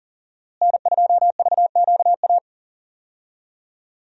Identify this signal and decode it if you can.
Morse code, heard as a keyed tone, about 40 wpm, 700 Hz. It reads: N2VQA
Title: USA Call Signs - 5 Characters - Repeated 2x - 40wpm